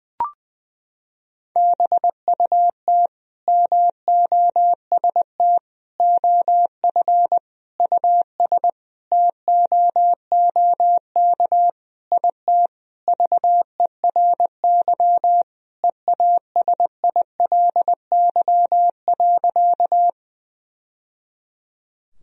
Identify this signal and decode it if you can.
Morse code, heard as a keyed tone, about 21 wpm, 700 Hz. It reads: BUT MOST OF US TOOK IT VERY EASILY.
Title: Sentences from Top 1000 Words - Review Part 1 - 20wpm